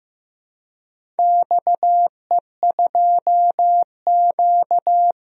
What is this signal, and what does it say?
Morse code, heard as a keyed tone, about 15 wpm, 700 Hz. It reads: XE2Q
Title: DXCC-100 Call Signs - 4 Characters - Repeated 3x - 15wpm